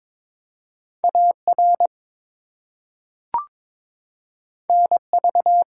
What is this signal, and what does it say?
Morse code, heard as a keyed tone, about 22 wpm, 700 Hz. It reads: AR NV
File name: ARRL Sections v2 - 22wpm